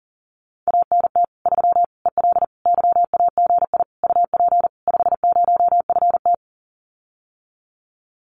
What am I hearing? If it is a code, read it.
Morse code, heard as a keyed tone, about 43 wpm, 700 Hz. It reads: ANT 3 EL YAGI UP 50FT
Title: Common QSO Phrases Encore 40wpm